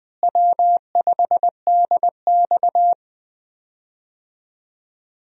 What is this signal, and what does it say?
Morse code, heard as a keyed tone, about 20 wpm, 700 Hz. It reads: W5DX
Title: USA Call Signs - 4 Characters - Repeated 2x - 20wpm